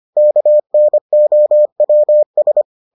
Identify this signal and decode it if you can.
Morse code, heard as a keyed tone, about 25 wpm, 600 Hz. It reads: KNOWS